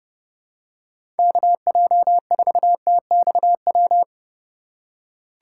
Morse code, heard as a keyed tone, about 32 wpm, 700 Hz. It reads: KJ4TXW
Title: USA Call Signs - 6 Characters - 2x Character Spacing - 30wpm